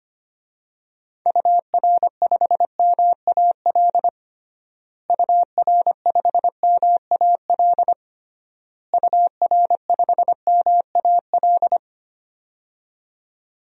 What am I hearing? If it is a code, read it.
Morse code, heard as a keyed tone, about 25 wpm, 700 Hz. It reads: UR5MAL UR5MAL UR5MAL
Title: DXCC-100 Call Signs - 6 Characters - Repeated 3x - 25wpm